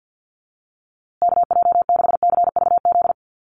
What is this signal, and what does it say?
Morse code, heard as a keyed tone, about 50 wpm, 700 Hz. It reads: KJ6CVZ
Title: USA Call Signs Repeated 2x 50wpm